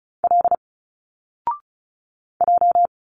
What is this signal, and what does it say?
Morse code, heard as a keyed tone, about 35 wpm, 700 Hz. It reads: L J
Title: Single Letters 35wpm